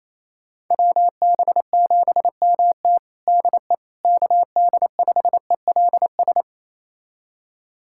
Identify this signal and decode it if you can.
Morse code, heard as a keyed tone, about 28 wpm, 700 Hz. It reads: WB7MT DE KD5ELS
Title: USA Callsign DE Callsign 28wpm